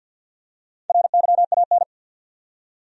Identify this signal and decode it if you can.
Morse code, heard as a keyed tone, about 53 wpm, 700 Hz. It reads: RYAN
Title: USA Common Names - Max 4 Letters - 50wpm